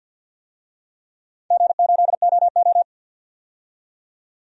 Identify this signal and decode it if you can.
Morse code, heard as a keyed tone, about 51 wpm, 700 Hz. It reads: G8OO
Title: DXCC-100 Call Signs - 4 Characters - 50wpm